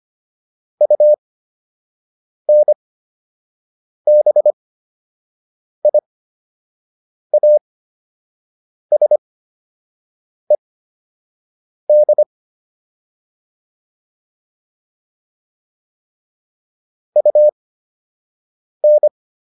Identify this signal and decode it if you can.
Morse code, heard as a keyed tone, about 25 wpm, spaced as for 6 wpm, 600 Hz. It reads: UNBIASED UN